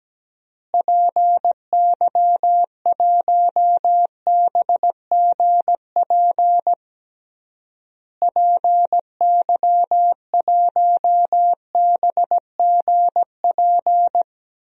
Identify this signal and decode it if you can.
Morse code, heard as a keyed tone, about 17 wpm, 700 Hz. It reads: PY1BGP PY1BGP
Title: DXCC-100 Call Signs - 6 Characters - Repeated 3x - 17wpm